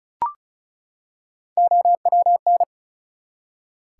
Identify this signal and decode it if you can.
Morse code, heard as a keyed tone, about 36 wpm, 700 Hz. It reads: OWN
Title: Three Letter Words - 35wpm